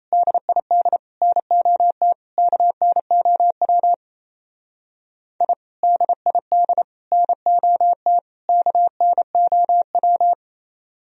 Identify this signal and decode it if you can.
Morse code, heard as a keyed tone, about 33 wpm, 700 Hz. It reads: DID NOT KNOW I DID NOT KNOW
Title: Speed-Racing - Sets of 4 Words 28wpm